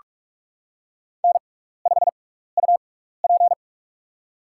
Morse code, heard as a keyed tone, about 46 wpm, spaced as for 16 wpm, 700 Hz. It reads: N5UP
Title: USA Call Signs - 6x Character Spacing - 45wpm